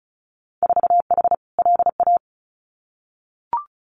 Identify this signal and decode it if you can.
Morse code, heard as a keyed tone, about 34 wpm, 700 Hz. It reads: 4H LA
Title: Winter Field Day Exchanges Encore 35wpm